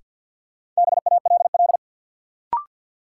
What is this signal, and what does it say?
Morse code, heard as a keyed tone, about 52 wpm, 700 Hz. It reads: BALL
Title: Sentences from Top 900 Words - New Word Drill - 50wpm